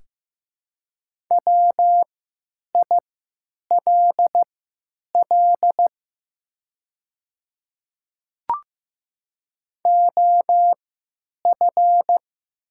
Morse code, heard as a keyed tone, about 15 wpm, spaced as for 8 wpm, 700 Hz. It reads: WILL OF
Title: 3x Character Spacing - Top 100 Words - 15wpm